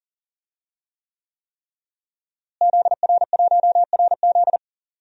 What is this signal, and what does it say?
Morse code, heard as a keyed tone, about 40 wpm, 700 Hz. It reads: ZR1RZ
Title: DXCC-100 Call Signs - 5 Characters - Repeated 3x - 40wpm